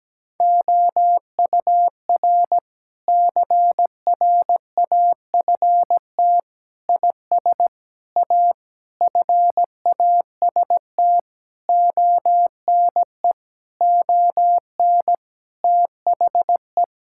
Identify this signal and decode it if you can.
Morse code, heard as a keyed tone, about 17 wpm, 700 Hz. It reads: OUR CRAFT IS A FAST ONE ON THE
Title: Sentences from Top 1000 Words - Review Part 6 - 17wpm